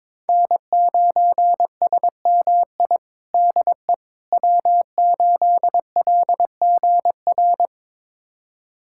Morse code, heard as a keyed tone, about 22 wpm, 700 Hz. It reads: N9SMI DE W8LGR
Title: USA Callsign DE Callsign - 5 Characters - 22wpm